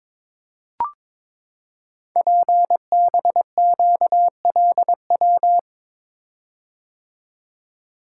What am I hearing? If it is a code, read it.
Morse code, heard as a keyed tone, about 22 wpm, 700 Hz. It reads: PBQLW
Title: Code Groups - 5 Letters - 22wpm